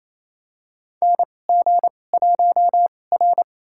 Morse code, heard as a keyed tone, about 27 wpm, spaced as for 20 wpm, 700 Hz. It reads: NG1R
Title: USA Call Signs - 4 Characters - 2x Character Spacing - 28wpm